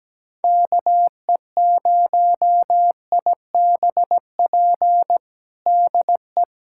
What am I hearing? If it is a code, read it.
Morse code, heard as a keyed tone, about 17 wpm, 700 Hz. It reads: KE0IBP DE
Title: USA Callsign DE Callsign - 6 Characters - 17wpm